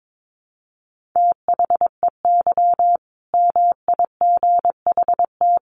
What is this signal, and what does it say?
Morse code, heard as a keyed tone, about 22 wpm, 700 Hz. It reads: THEY MIGHT
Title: Sentences from Top 1100 Words 22wpm